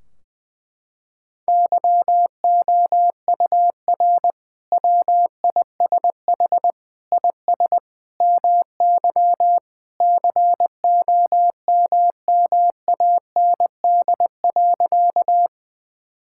Morse code, heard as a keyed tone, about 20 wpm, 700 Hz. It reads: YOUR WISH IS MY COMMAND.
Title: Sentences from Top 1600 Words 20wpm